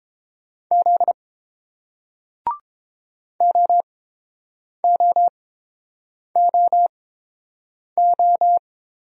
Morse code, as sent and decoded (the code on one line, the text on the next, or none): --.. / --- --- --- ---
Z OOOO